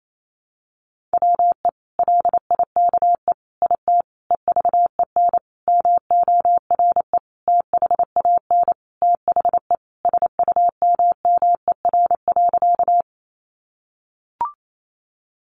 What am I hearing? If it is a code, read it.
Morse code, heard as a keyed tone, about 28 wpm, 700 Hz. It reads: WE LIKE IT EVEN MORE THAN THE SUMMER.
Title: Sentences from Top 1100 Words 28wpm